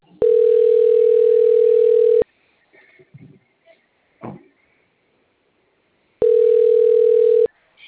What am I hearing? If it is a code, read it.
Ringback tone (North America), 440 + 480 Hz.
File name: out- - - - - .14494